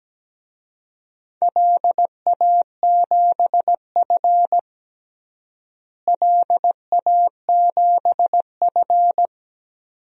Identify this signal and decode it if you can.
Morse code, heard as a keyed tone, about 17 wpm, 700 Hz. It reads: LA7F LA7F
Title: DXCC-100 Call Signs - 4 Characters - Repeated 2x - 17wpm